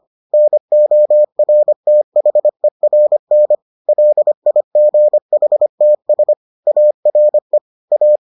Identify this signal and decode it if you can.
Morse code, heard as a keyed tone, about 25 wpm, 600 Hz. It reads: NORTHERN LIGHTS ARE A